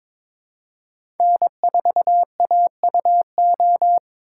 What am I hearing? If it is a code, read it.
Morse code, heard as a keyed tone, about 21 wpm, 700 Hz. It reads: N4AUO
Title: USA Call Signs - 5 Characters - 4x Character Spacing - 22wpm